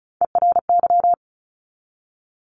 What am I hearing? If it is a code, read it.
Morse code, heard as a keyed tone, about 36 wpm, 700 Hz. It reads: ERY
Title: Top Three Letter Combinations in Words 35wpm